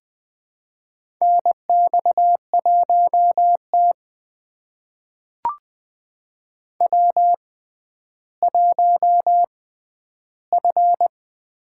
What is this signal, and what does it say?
Morse code, heard as a keyed tone, about 20 wpm, spaced as for 11 wpm, 700 Hz. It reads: NX1T W1F
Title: USA Call Signs - 4 Characters - 6x Character Spacing - 20wpm